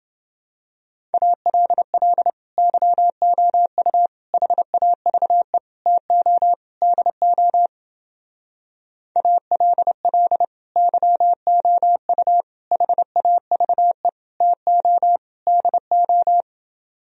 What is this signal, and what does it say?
Morse code, heard as a keyed tone, about 30 wpm, 700 Hz. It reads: ALL YOU HAVE TO DO ALL YOU HAVE TO DO